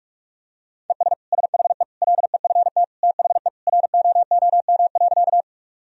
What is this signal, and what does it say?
Morse code, heard as a keyed tone, about 46 wpm, 700 Hz. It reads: ES SHE LEFT THE ROOM.
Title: Sentences from Top 300 Words and Abbreviations 45wpm